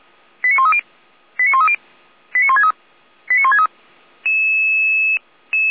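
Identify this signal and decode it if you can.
Five-tone selective call (ZVEI), heard as quick runs of five tones, tones of about 70 ms. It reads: 89130 89130 89163 89163
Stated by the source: ZVEI five-tone call received on an SDR.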